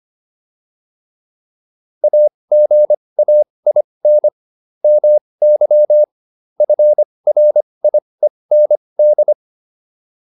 Morse code, heard as a keyed tone, about 25 wpm, spaced as for 20 wpm, 600 Hz. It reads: AGAIN MY FRIEND